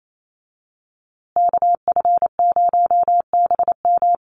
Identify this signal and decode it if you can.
Morse code, heard as a keyed tone, about 28 wpm, 700 Hz. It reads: KF0BM
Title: USA Call Signs Repeated 2x 28wpm